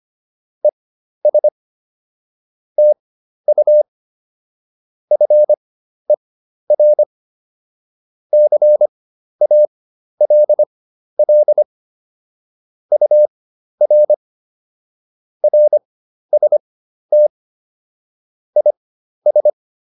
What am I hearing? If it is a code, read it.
Morse code, heard as a keyed tone, about 25 wpm, spaced as for 12 wpm, 600 Hz. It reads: ES TU FER CALL UR RST IS